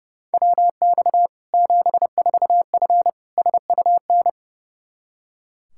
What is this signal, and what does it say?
Morse code, heard as a keyed tone, about 30 wpm, 700 Hz. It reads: WX 74F SUN
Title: Practice QSO - Exchange 2 - Info - 30wpm